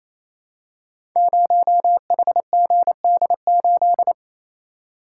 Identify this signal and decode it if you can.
Morse code, heard as a keyed tone, about 27 wpm, 700 Hz. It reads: 0HGD8
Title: Code Groups - 5 Letter:Numbers - 28wpm